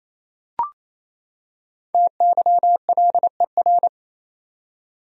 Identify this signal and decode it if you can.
Morse code, heard as a keyed tone, about 28 wpm, 700 Hz. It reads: TYLER
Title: USA Common Names - 28wpm